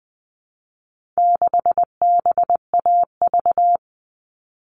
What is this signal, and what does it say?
Morse code, heard as a keyed tone, about 20 wpm, 700 Hz. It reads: 6BAV
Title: All Cut Numbers - 4-Digit 20wpm